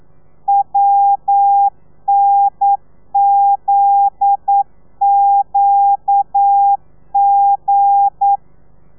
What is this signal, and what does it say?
Morse code, heard as a keyed tone, about 9 wpm, 800 Hz. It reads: WNZQG
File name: telegrafia / i2tfj / Lez9 40